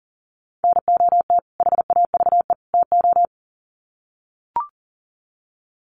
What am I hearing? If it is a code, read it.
Morse code, heard as a keyed tone, about 40 wpm, 700 Hz. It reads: NOT HAVE TO
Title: Sets of 3 Words 40wpm